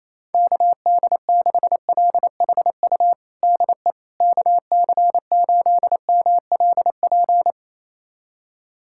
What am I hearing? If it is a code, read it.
Morse code, heard as a keyed tone, about 28 wpm, 700 Hz. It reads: KD6LHU DE KC8MLP
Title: USA Callsign DE Callsign 28wpm